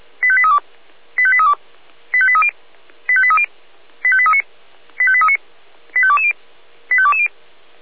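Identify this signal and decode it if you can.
Five-tone selective call (ZVEI), heard as quick runs of five tones, tones of about 70 ms. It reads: 85832 85832 85830 85830 85839 85839 85220 85220
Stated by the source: ZVEI five-tone call received on SDR.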